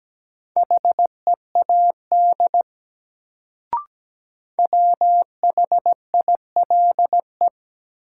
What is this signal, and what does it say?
Morse code, heard as a keyed tone, about 17 wpm, 700 Hz. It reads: HEAD WHILE
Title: Sentences from Top 300 Words - New Word Drill - 17wpm